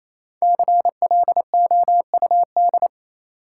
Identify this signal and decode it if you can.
Morse code, heard as a keyed tone, about 28 wpm, 700 Hz. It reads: CLOUD
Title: Five Letter Words - 28wpm